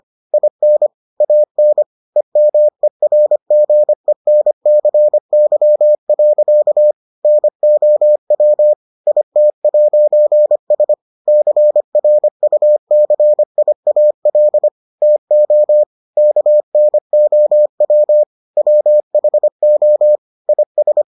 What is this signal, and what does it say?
Morse code, heard as a keyed tone, about 25 wpm, 600 Hz. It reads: IN AN EMERGENCY. NOW IT'S CRUCIAL TO KNOW WHO IS